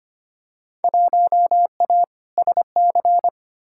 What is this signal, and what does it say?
Morse code, heard as a keyed tone, about 25 wpm, 700 Hz. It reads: 1A SC
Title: Field Day Exchanges - 25wpm